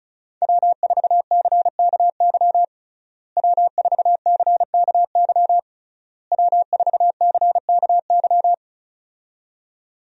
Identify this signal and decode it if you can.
Morse code, heard as a keyed tone, about 36 wpm, 700 Hz. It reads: W4CKY W4CKY W4CKY
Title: USA Call Signs - 5 Characters - Repeated 3x - 35wpm